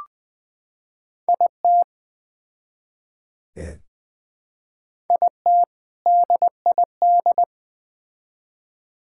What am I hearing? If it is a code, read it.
Morse code, heard as a keyed tone, about 21 wpm, 700 Hz. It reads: IT IT DID
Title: Build A Phrase - 3-Word Phrases - 20wpm